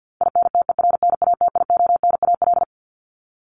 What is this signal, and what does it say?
Morse code, heard as a keyed tone, about 50 wpm, 700 Hz. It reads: INTERNATIONAL
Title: Top 501-750 Words 50wpm